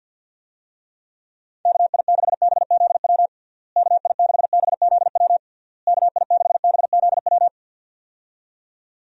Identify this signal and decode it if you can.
Morse code, heard as a keyed tone, about 51 wpm, 700 Hz. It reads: KI6BZW KI6BZW KI6BZW
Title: USA Call Signs - 6 Characters - Repeated 3x - 50wpm